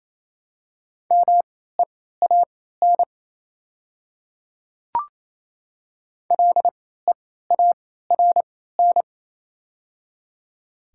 Morse code, heard as a keyed tone, about 28 wpm, spaced as for 16 wpm, 700 Hz. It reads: MEAN LEARN
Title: 3x Character Spacing - Top 300 Words - 28wpm